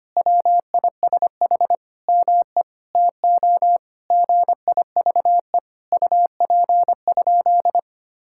Morse code, heard as a keyed tone, about 25 wpm, 700 Hz. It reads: WISH ME TO GIVE UP?
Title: Sentences from Top 1600 Words 25wpm